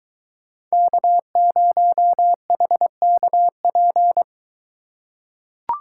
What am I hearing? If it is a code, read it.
Morse code, heard as a keyed tone, about 22 wpm, 700 Hz. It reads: K0HKP